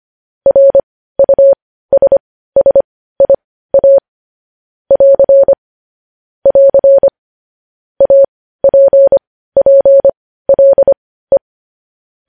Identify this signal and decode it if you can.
Morse code, heard as a keyed tone, about 25 wpm, spaced as for 15 wpm, 550 Hz. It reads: RUSSIA <AR> <AR> APPLE
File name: podcast / News headlines-1x-25x15